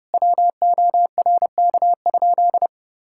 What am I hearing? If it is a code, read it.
Morse code, heard as a keyed tone, about 28 wpm, 700 Hz. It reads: WORK?
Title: Sentences from Top 1000 Words - Review Part 6 - 30wpm